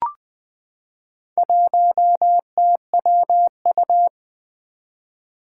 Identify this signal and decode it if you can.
Morse code, heard as a keyed tone, about 20 wpm, 700 Hz. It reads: 1TWU
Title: Four Challenging Letter-Number Combinations 20wpm